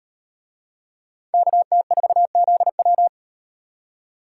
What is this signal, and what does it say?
Morse code, heard as a keyed tone, about 38 wpm, 700 Hz. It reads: KT4ZW